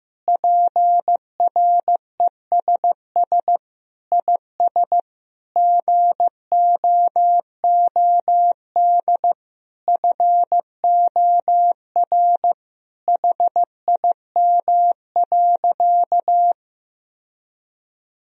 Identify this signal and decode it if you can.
Morse code, heard as a keyed tone, about 16 wpm, 700 Hz. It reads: PRESS IS GOOD FOR HIM.
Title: Sentences from Top 1000 Words - Review Part 1 - 15wpm